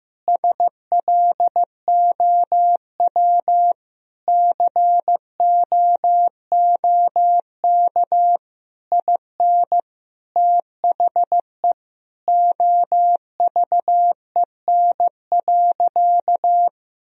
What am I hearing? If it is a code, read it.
Morse code, heard as a keyed tone, about 16 wpm, 700 Hz. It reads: SLOW COOK IN THE OVEN.
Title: Sentences from Top 1000 Words - Review Part 4 - 15wpm